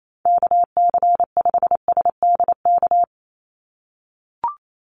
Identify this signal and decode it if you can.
Morse code, heard as a keyed tone, about 28 wpm, 700 Hz. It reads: KC5SDK
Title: USA Call Signs 28wpm